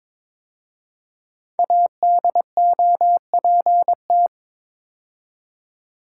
Morse code, heard as a keyed tone, about 22 wpm, 700 Hz. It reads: ADOPT
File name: 5 Letter Words - First 3 Letters Given - 22wpm